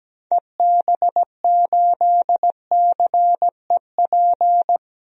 Morse code, heard as a keyed tone, about 17 wpm, 700 Hz. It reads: EB8CEP
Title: DXCC-100 Call Signs - 6 Characters - Repeated 3x - 17wpm